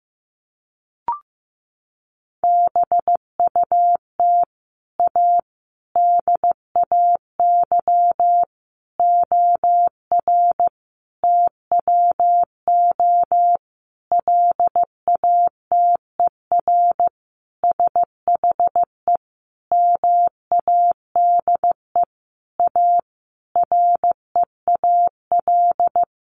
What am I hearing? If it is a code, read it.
Morse code, heard as a keyed tone, about 15 wpm, 700 Hz. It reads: BUT A DAY OR TWO LATER SHE MADE A REAL
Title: Sentences from Top 1100 Words 15wpm